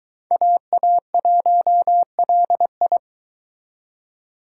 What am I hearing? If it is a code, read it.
Morse code, heard as a keyed tone, about 22 wpm, 700 Hz. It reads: AA1LI